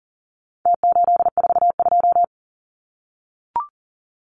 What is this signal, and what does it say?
Morse code, heard as a keyed tone, about 40 wpm, 700 Hz. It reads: T842
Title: Numbers - Contest Style - 4-Digit 40wpm